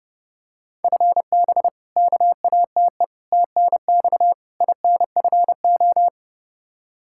Morse code, heard as a keyed tone, about 30 wpm, 700 Hz. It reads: FB KATE TNX INFO
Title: Practice QSO - Exchange 3 - More Info - 30wpm